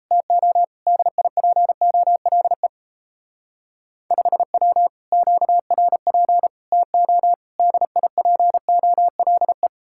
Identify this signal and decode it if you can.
Morse code, heard as a keyed tone, about 33 wpm, 700 Hz. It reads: TO DIPOLE 5W QRP TO DIPOLE
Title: Speed-Racing Common QSO Phrases - 28wpm